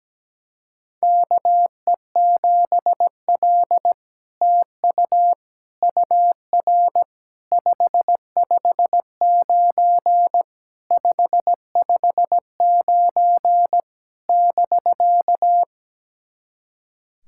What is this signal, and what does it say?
Morse code, heard as a keyed tone, about 17 wpm, 700 Hz. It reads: KE7L TU UR 559 559 <BK>
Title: SOTA QSO Practice 17wpm